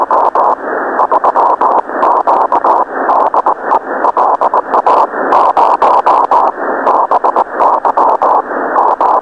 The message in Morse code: none